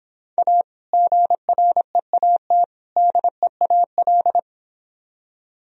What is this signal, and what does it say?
Morse code, heard as a keyed tone, about 25 wpm, 700 Hz. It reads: A GREAT DEAL